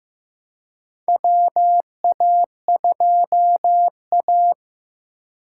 Morse code, heard as a keyed tone, about 15 wpm, 700 Hz. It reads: WA2A
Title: USA Call Signs - 3x Character Spacing - 15wpm